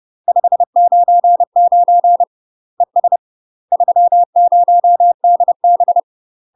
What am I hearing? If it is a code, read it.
Morse code, heard as a keyed tone, about 30 wpm, 700 Hz. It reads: 599 ES 30DB
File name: UK MASTER QSOs TRAINING / 30WPM / UK MASTER QSOs 0018